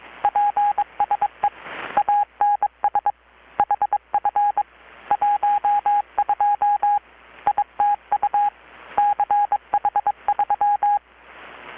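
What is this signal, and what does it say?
Morse code, heard as a keyed tone, about 23 wpm, 820 Hz. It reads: PSE ANS HF 12 ITU CH3
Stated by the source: Morse from KPH's Night of Nights XX maritime radiotelegraphy event, received in Chiapas, Mexico about 3,800 km away on a Kenwood R-600 with a grounded 10 m long-wire antenna.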